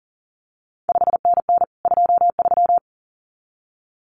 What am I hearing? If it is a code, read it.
Morse code, heard as a keyed tone, about 39 wpm, 700 Hz. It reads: 5NN 23
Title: CQ WW DX Contest Exchange Encore - 40wpm